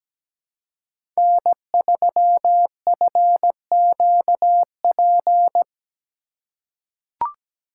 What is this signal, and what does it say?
Morse code, heard as a keyed tone, about 17 wpm, 700 Hz. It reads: N3FQP
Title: USA Call Signs - 5 Characters - 2x Character Spacing - 17wpm